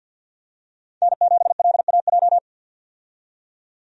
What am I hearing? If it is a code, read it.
Morse code, heard as a keyed tone, about 52 wpm, 700 Hz. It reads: N7LAJ
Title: USA Call Signs - 5 Characters - 50wpm